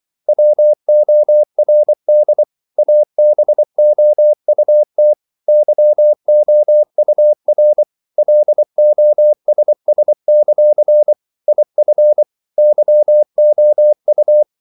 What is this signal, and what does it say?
Morse code, heard as a keyed tone, about 24 wpm, 600 Hz. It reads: WORD ABOUT YOUR LOSS; IF YOU
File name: CW / Practice / If24